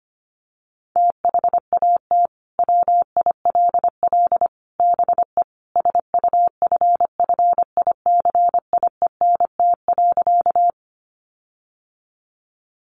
Morse code, heard as a keyed tone, about 25 wpm, 700 Hz. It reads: THAT WILL BE SUFFICIENT.
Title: Sentences from Top 1400 Words 25wpm